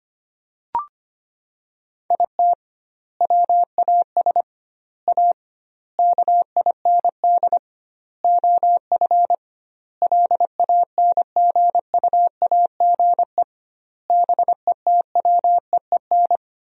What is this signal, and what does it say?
Morse code, heard as a keyed tone, about 25 wpm, 700 Hz. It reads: IT WAS A KIND OF LANGUAGE BETWEEN
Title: 2x Word-Spacing - Sentences from Top 500 Words - 25wpm